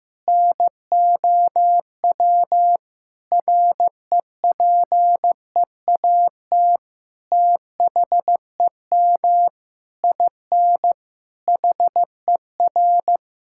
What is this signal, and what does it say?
Morse code, heard as a keyed tone, about 15 wpm, 700 Hz. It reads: NOW REPEAT THEM IN HER
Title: Sentences from Top 1600 Words 15wpm